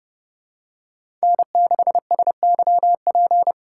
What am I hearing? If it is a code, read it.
Morse code, heard as a keyed tone, about 30 wpm, 700 Hz. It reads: N6SYP
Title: USA Call Signs - 5 Characters - 2x Character Spacing - 30wpm